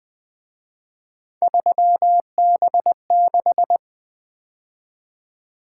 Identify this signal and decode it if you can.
Morse code, heard as a keyed tone, about 24 wpm, 700 Hz. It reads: 3B6
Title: Speed-Racing - Three Letter-Number Combinations - 20wpm